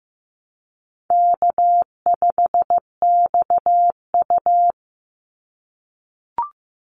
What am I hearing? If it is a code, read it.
Morse code, heard as a keyed tone, about 15 wpm, 700 Hz. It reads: K5XU
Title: Easy Call Signs - 15wpm